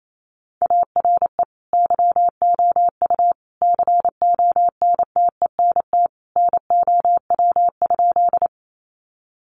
Morse code, heard as a keyed tone, about 28 wpm, 700 Hz. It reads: ARE YOU CONTENT NOW?
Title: Sentences from Top 900 Words 28wpm